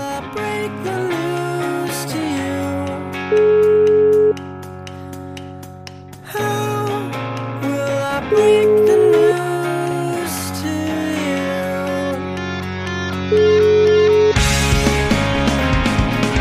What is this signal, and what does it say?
Ringback tone (Europe, 425 Hz), with noise.